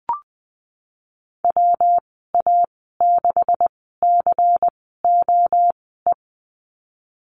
Morse code, heard as a keyed tone, about 19 wpm, spaced as for 14 wpm, 700 Hz. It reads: WA6COE
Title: 173 Lesson 28g - Number 9 - Call Signs 20wpm